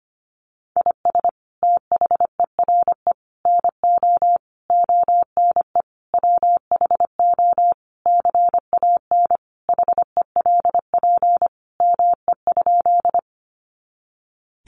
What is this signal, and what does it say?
Morse code, heard as a keyed tone, about 25 wpm, 700 Hz. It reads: IS THERE NO ONE WHO CAN HELP ME?
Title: Sentences from Top 200 Words 25wpm